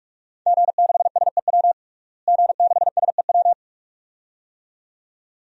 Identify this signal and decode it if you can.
Morse code, heard as a keyed tone, about 46 wpm, 700 Hz. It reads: G6SEW G6SEW
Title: DXCC-100 Call Signs - 5 Characters - Repeated 2x - 45wpm